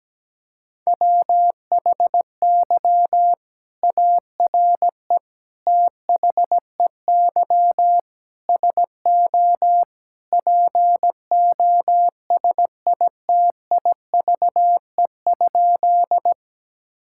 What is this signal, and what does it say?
Morse code, heard as a keyed tone, about 17 wpm, 700 Hz. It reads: WHY ARE THEY SO POSITIVE?
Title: Sentences from Top 1000 Words - Review Part 5 - 17wpm